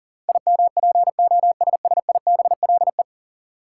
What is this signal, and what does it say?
Morse code, heard as a keyed tone, about 40 wpm, 700 Hz. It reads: IMPOSSIBLE
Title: Sentences from Top 1100 Words - New Word Drill Alt - 40wpm